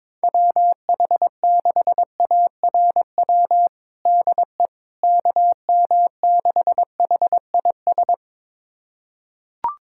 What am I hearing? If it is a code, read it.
Morse code, heard as a keyed tone, about 21 wpm, 700 Hz. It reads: WH6ARW DE KM6HIS
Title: USA Callsign DE Callsign - 6 Characters - 22wpm